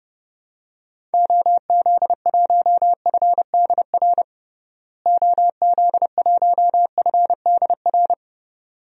Morse code, heard as a keyed tone, about 30 wpm, 700 Hz. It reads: OZ1FDR OZ1FDR
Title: DXCC-100 Call Signs - 6 Characters - Repeated 2x - 30wpm